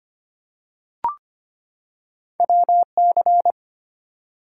Morse code, heard as a keyed tone, about 25 wpm, 700 Hz. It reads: WC
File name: Two Letter-Number Combinations - 25wpm